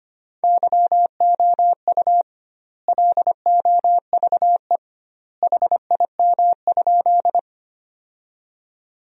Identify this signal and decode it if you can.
Morse code, heard as a keyed tone, about 25 wpm, 700 Hz. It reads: YOU LOVE HIM?
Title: 2x Word-Spacing - Sentences from Top 400 Words - 25wpm